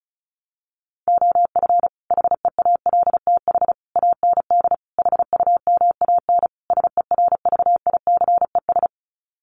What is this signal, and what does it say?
Morse code, heard as a keyed tone, about 34 wpm, 700 Hz. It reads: OF HEALTH AND HUMAN SERVICES
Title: Sets of 5 Words 35wpm